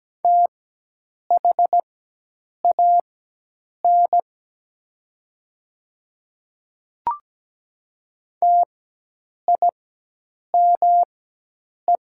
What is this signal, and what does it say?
Morse code, heard as a keyed tone, about 17 wpm, spaced as for 8 wpm, 700 Hz. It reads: THAN TIME
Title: 4x Character Spacing - Top 100 Words - 17wpm